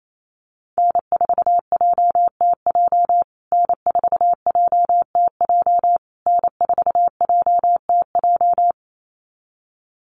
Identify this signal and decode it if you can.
Morse code, heard as a keyed tone, about 28 wpm, 700 Hz. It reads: N4JTJ N4JTJ N4JTJ
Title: USA Call Signs Repeated 3x 28wpm